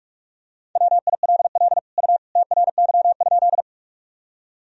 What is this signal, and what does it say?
Morse code, heard as a keyed tone, about 41 wpm, 700 Hz. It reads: WILL U TRY?
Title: Sentences from Top 200 Words and Abbreviations 45wpm